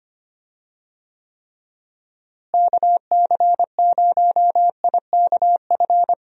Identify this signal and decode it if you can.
Morse code, heard as a keyed tone, about 25 wpm, 700 Hz. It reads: KC0IKF
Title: USA Call Signs - 6 Characters - Repeated 3x - 25wpm